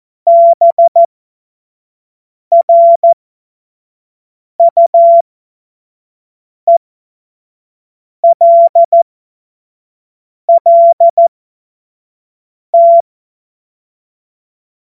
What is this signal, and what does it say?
Morse code, heard as a keyed tone, about 14 wpm, spaced as for 5 wpm, 680 Hz. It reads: BRUELLT
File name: Morsekurs / Witze / Witz 08. Februar 2022 25 F70